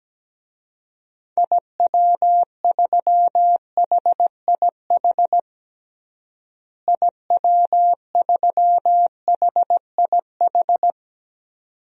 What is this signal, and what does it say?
Morse code, heard as a keyed tone, about 17 wpm, 700 Hz. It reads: IW3HIH IW3HIH
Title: DXCC-100 Call Signs - 6 Characters - Repeated 2x - 17wpm